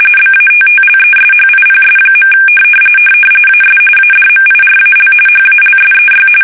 RTTY, about 45.45 baud. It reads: BATTLESHIPS. AS ADMIRAL HALSEY
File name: fsk850 / battleshipnj